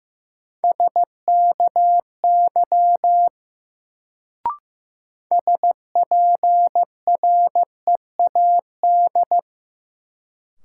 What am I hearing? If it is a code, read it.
Morse code, heard as a keyed tone, about 15 wpm, 700 Hz. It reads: SKY SPREAD
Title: Sentences from Top 1200 Words - New Word Drill Alt - 15wpm